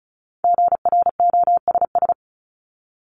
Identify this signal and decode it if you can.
Morse code, heard as a keyed tone, about 34 wpm, 700 Hz. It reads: GROSS
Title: Top 1751-2000 Words 35wpm